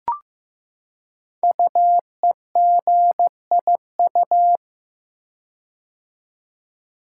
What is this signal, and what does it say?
Morse code, heard as a keyed tone, about 15 wpm, 700 Hz. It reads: UEGIU
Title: Code Groups - 5 Letters - 15wpm